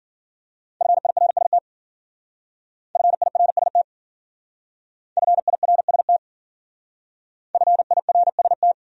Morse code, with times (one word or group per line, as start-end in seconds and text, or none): 0.8-1.6 FIRST
3.0-3.8 FIRST
5.2-6.2 FIRST
7.5-8.7 FIRST